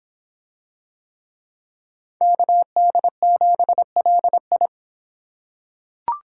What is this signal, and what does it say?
Morse code, heard as a keyed tone, about 25 wpm, 700 Hz. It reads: KD7LI